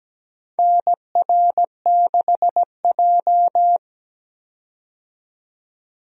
Morse code, heard as a keyed tone, about 17 wpm, 700 Hz. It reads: NR6J
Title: USA Call Signs - 4 Characters - 17wpm